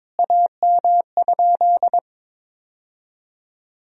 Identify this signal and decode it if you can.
Morse code, heard as a keyed tone, about 22 wpm, 700 Hz. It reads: AM?
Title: Sentences from Top 1000 Words - Review Part 2 - 22wpm